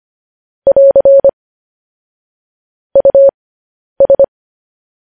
Morse code, heard as a keyed tone, about 25 wpm, spaced as for 10 wpm, 550 Hz. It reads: <AR> US